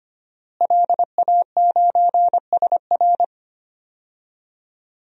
Morse code, heard as a keyed tone, about 25 wpm, 700 Hz. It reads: LA9SR
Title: DXCC-100 Call Signs - 5 Characters - Repeated 3x - 25wpm